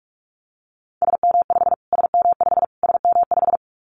Morse code, heard as a keyed tone, about 44 wpm, 700 Hz. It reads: SM5 SM5 SM5
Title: Callsign Prefixes for Top 25 DXCC - Repeated 3x - 45wpm